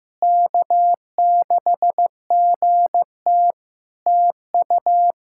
Morse code, heard as a keyed tone, about 16 wpm, 700 Hz. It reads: K6GT TU
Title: Practice QSO - Exchange 4 - Ending - 15wpm